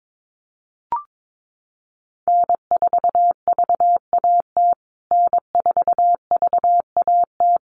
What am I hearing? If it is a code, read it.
Morse code, heard as a keyed tone, about 22 wpm, 700 Hz. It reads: N4VAT N4VAT
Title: USA Call Signs Repeated 3x 22wpm